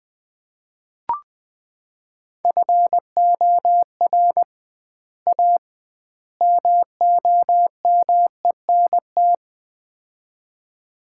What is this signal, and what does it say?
Morse code, heard as a keyed tone, about 20 wpm, 700 Hz. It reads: FOR A MOMENT
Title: 2x Word-Spacing - Sets of 3 Words - 20wpm